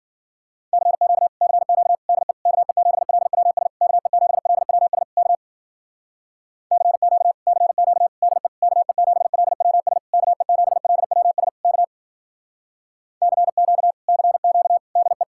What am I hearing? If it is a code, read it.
Morse code, heard as a keyed tone, about 52 wpm, 700 Hz. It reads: CQ CQ DE KE7LWS KE7LWS K CQ CQ DE KE7LWS KE7LWS K CQ CQ DE